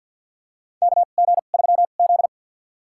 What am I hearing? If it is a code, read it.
Morse code, heard as a keyed tone, about 47 wpm, spaced as for 35 wpm, 700 Hz. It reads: KG3Z
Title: USA Call Signs - 4 Characters - 2x Character Spacing - 50wpm